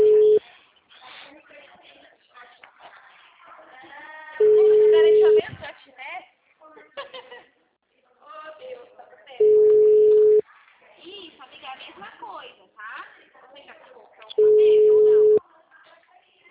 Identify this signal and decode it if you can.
Ringback tone (Europe, 425 Hz), with speech.
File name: out- -2012- - - .288